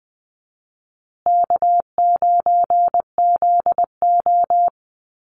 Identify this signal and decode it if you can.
Morse code, heard as a keyed tone, about 19 wpm, 700 Hz. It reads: K9ZO
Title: Easy Call Signs Repeated 3x - 20wpm